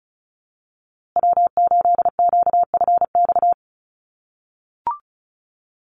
Morse code, heard as a keyed tone, about 34 wpm, 700 Hz. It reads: W8QFX
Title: USA Call Signs Repeated 2x 35wpm